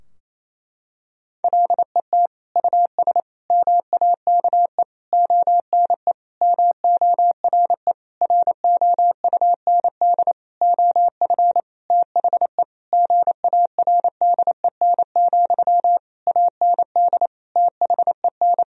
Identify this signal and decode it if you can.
Morse code, heard as a keyed tone, about 28 wpm, 700 Hz. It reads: LET US MAKE ONE MORE ROUND OF THE GARDEN, AND THEN
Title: Sentences from Top 1000 Words - Review Part 4 - 28wpm